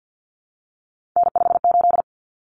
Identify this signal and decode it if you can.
Morse code, heard as a keyed tone, about 48 wpm, 700 Hz. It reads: N58
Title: Common Cut Numbers - 3-Digit 50wpm